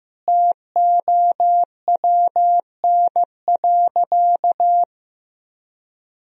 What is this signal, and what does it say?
Morse code, heard as a keyed tone, about 15 wpm, 700 Hz. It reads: TOWN.
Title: Sentences from Top 1000 Words - Review Part 2 - 15wpm